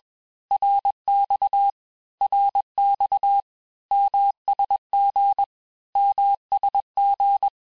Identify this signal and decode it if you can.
Morse code, heard as a keyed tone, about 21 wpm, 800 Hz. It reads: RX RX MSG MSG
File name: cours-du-lundi files / cours-1592--teste-abreviations-vitesse-15mpm-puis-20mpm